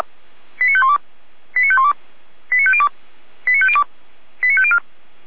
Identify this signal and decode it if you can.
Five-tone selective call (ZVEI), heard as quick runs of five tones, tones of about 70 ms. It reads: 89512 89512 89502 89502 89504
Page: SDR/ZVEI